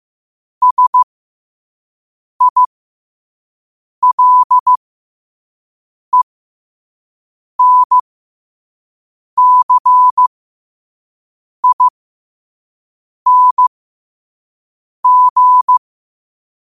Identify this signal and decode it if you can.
Morse code, heard as a keyed tone, about 15 wpm, spaced as for 5 wpm, 1000 Hz. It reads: SILENCING
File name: qotdfiles / 5wpm